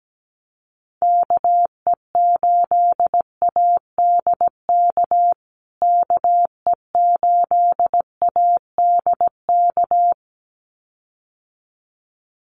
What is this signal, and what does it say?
Morse code, heard as a keyed tone, about 17 wpm, 700 Hz. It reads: KE8ADK KE8ADK
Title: USA Call Signs Repeated 2x 17wpm